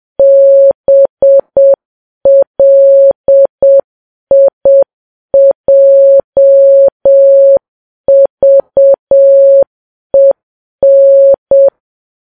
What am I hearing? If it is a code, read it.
Morse code, heard as a keyed tone, about 7 wpm, 550 Hz. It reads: BLIJVEN
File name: Files Web tls / Morse on9fv / morse7kps